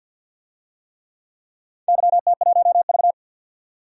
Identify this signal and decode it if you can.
Morse code, heard as a keyed tone, about 53 wpm, 700 Hz. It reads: YT1V
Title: DXCC-100 Call Signs - 4 Characters - 50wpm